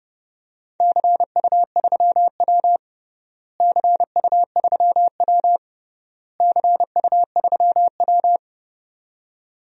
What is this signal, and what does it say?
Morse code, heard as a keyed tone, about 28 wpm, 700 Hz. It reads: CU3W CU3W CU3W
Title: DXCC-100 Call Signs - 4 Characters - Repeated 3x - 30wpm